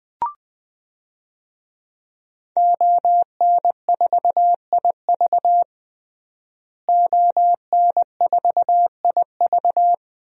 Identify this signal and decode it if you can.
Morse code, heard as a keyed tone, about 20 wpm, 700 Hz. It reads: ON4IV ON4IV
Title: DXCC-100 Call Signs - 5 Characters - Repeated 3x - 20wpm